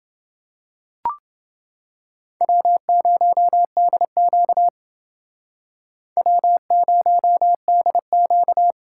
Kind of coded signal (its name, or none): Morse code